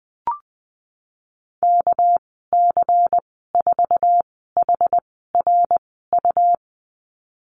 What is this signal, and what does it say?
Morse code, heard as a keyed tone, about 20 wpm, spaced as for 15 wpm, 700 Hz. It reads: KC4HRU
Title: 219 Lesson 35g - Number 0 - Call Signs 20wpm